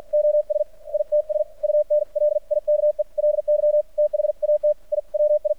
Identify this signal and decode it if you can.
Morse code, heard as a keyed tone, about 52 wpm, spaced as for 44 wpm, 600 Hz. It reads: OS ITS UNLIKELY THAT I WI
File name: archived / CRSnet